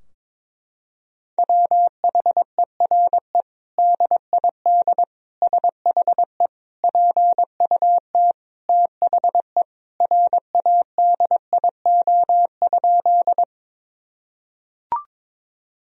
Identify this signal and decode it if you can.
Morse code, heard as a keyed tone, about 22 wpm, 700 Hz. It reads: WHERE DID SHE PUT THE RADIO?
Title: Sentences from Top 1000 Words - Review Part 3 - 22wpm